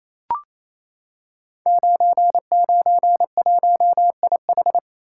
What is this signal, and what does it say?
Morse code, heard as a keyed tone, about 28 wpm, 700 Hz. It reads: 991IH
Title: Code Groups - 5 Letter:Numbers - 28wpm